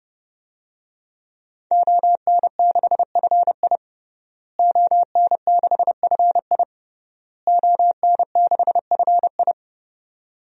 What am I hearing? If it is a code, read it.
Morse code, heard as a keyed tone, about 32 wpm, 700 Hz. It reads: ON6FI ON6FI ON6FI
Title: DXCC-100 Call Signs - 5 Characters - Repeated 3x - 30wpm